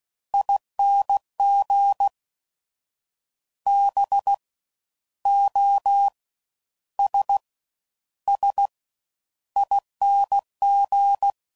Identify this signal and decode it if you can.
Morse code, heard as a keyed tone, about 16 wpm, 800 Hz. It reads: ING B O S S ING
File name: Suffixes / Special INGs / Fast Ings